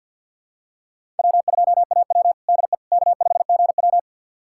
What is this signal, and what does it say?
Morse code, heard as a keyed tone, about 49 wpm, 700 Hz. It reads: W2AW DE K5GW
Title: USA Callsign DE Callsign - 4 Characters - 50wpm